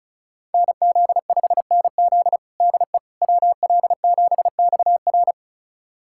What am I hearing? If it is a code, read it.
Morse code, heard as a keyed tone, about 34 wpm, 700 Hz. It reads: NZ5NZ DE WL7XR
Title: USA Callsign DE Callsign - 5 Characters - 35wpm